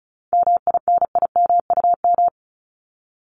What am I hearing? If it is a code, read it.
Morse code, heard as a keyed tone, about 34 wpm, 700 Hz. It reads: MINIMUM
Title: Top 1251-1500 Words 35wpm